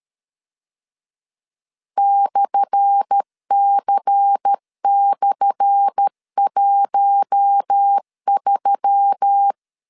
Morse code, heard as a keyed tone, about 13 wpm, 800 Hz. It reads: /C/13
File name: LEZIONI / 10 SIGNS / 10A SLASH 12wpm 2314